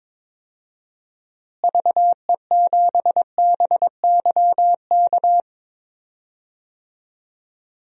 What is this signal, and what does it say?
Morse code, heard as a keyed tone, about 22 wpm, 700 Hz. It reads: VE7BYK
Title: DXCC-100 Call Signs - 6 Characters - Repeated 2x - 22wpm